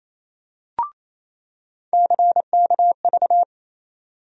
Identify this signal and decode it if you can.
Morse code, heard as a keyed tone, about 28 wpm, 700 Hz. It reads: CKV
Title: USA Call Sign Postfixes - 3 Letters 28wpm